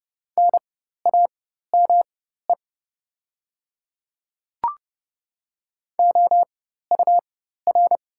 Morse code, heard as a keyed tone, about 30 wpm, spaced as for 14 wpm, 690 Hz. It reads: NAME OUR
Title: 4x Character Spacing - Top 500 Words - 30wpm